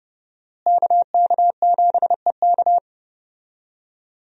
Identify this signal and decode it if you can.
Morse code, heard as a keyed tone, about 30 wpm, 700 Hz. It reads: KK7EK
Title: USA Call Signs - 5 Characters - Repeated 2x - 30wpm